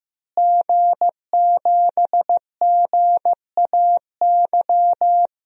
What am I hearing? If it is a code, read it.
Morse code, heard as a keyed tone, about 16 wpm, 700 Hz. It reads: G7GAY
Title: DXCC-100 Call Signs - 5 Characters - Repeated 3x - 15wpm